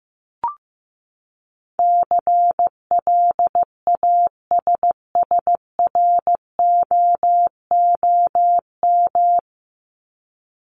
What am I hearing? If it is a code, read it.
Morse code, heard as a keyed tone, about 15 wpm, 700 Hz. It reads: CLASSROOM
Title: Top 1251-1500 Words 15wpm